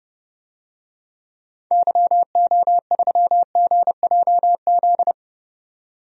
Morse code, heard as a keyed tone, about 30 wpm, 700 Hz. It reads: YO3GJZ
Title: DXCC-100 Call Signs - 6 Characters - 30wpm